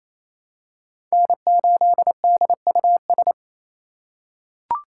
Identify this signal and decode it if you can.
Morse code, heard as a keyed tone, about 28 wpm, 700 Hz. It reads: N8DUS
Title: K1USN SST QSO Practice - 28wpm